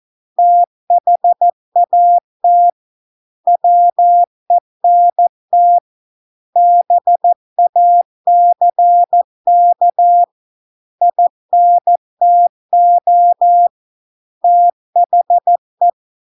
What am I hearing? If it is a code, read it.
Morse code, heard as a keyed tone, about 14 wpm, 700 Hz. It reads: THAT WENT BACK INTO THE